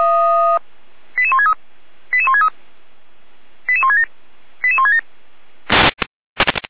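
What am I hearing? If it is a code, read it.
Five-tone selective call (ZVEI), heard as quick runs of five tones, tones of about 70 ms. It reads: 88163 88163 88167 88167
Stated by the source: ZVEI five-tone selective call received on SDR.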